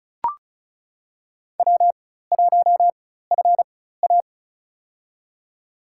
Morse code, heard as a keyed tone, about 34 wpm, spaced as for 16 wpm, 700 Hz. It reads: W1FA
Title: USA Call Signs - 4 Characters - 4x Character Spacing - 35wpm